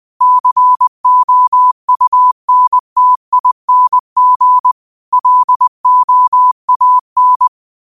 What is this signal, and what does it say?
Morse code, heard as a keyed tone, about 20 wpm, 1000 Hz. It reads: COUNTING LOAN
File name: qotdfiles / 20wpm